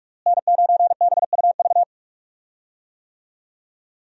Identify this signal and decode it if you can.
Morse code, heard as a keyed tone, about 46 wpm, 700 Hz. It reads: N9BUV
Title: USA Call Signs - 5 Characters - Repeated 3x - 45wpm